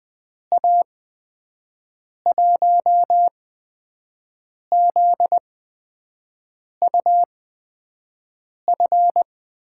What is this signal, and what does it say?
Morse code, heard as a keyed tone, about 20 wpm, spaced as for 5 wpm, 700 Hz. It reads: A1ZUF
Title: USA Call Signs - 6 Characters - 8x Character Spacing - 20wpm